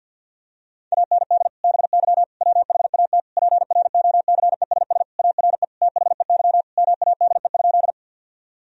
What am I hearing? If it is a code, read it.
Morse code, heard as a keyed tone, about 52 wpm, 700 Hz. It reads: AND BY WHAT PROCESS ARE THEY MADE?
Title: Sentences from Top 1000 Words - Review Part 3 - 50wpm